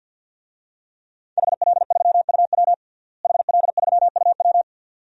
Morse code, heard as a keyed tone, about 53 wpm, 700 Hz. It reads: HL3UW HL3UW
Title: DXCC-100 Call Signs - 5 Characters - Repeated 2x - 50wpm